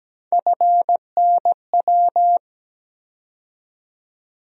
Morse code, heard as a keyed tone, about 17 wpm, 700 Hz. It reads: FNW
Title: USA Call Sign Postfixes - 3 Letters 17wpm